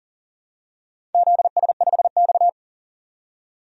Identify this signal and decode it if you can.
Morse code, heard as a keyed tone, about 40 wpm, 690 Hz. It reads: ZS5X
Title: DXCC-100 Call Signs - 4 Characters - 40wpm